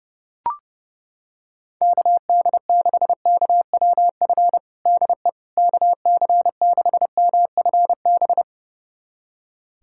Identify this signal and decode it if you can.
Morse code, heard as a keyed tone, about 32 wpm, 700 Hz. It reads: KD6KWF DE KC6MFB
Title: USA Callsign DE Callsign - 6 Characters - 30wpm